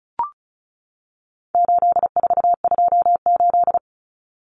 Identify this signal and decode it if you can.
Morse code, heard as a keyed tone, about 34 wpm, 700 Hz. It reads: 8428
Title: Numbers - 4-Digit 35wpm